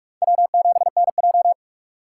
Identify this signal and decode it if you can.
Morse code, heard as a keyed tone, about 45 wpm, 700 Hz. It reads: W7NJ
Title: USA Call Signs - 4 Characters - 6x Character Spacing - 45wpm